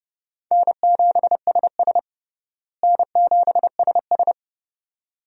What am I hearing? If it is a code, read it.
Morse code, heard as a keyed tone, about 32 wpm, 700 Hz. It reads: N7SS N7SS
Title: USA Call Signs - 4 Characters - Repeated 2x - 30wpm